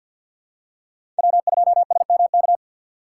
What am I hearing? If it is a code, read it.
Morse code, heard as a keyed tone, about 49 wpm, 700 Hz. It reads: W2SMK
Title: USA Call Signs - 3x Character Spacing - 50wpm